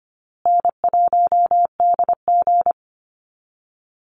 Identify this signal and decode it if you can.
Morse code, heard as a keyed tone, about 25 wpm, 700 Hz. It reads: N1DG
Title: Easy Call Signs Repeated 3x - 25wpm